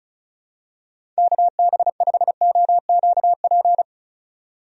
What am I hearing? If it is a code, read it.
Morse code, heard as a keyed tone, about 34 wpm, 700 Hz. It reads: KB5OQP
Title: USA Call Signs - 6 Characters - 35wpm